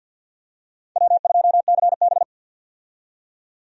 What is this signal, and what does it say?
Morse code, heard as a keyed tone, about 52 wpm, 700 Hz. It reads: W2CB
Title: USA Call Signs - 4 Characters - 50wpm